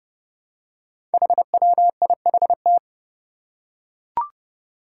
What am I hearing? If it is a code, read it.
Morse code, heard as a keyed tone, about 30 wpm, 700 Hz. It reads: HWIHT
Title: Five Challenging Letter-Number Combinations 30wpm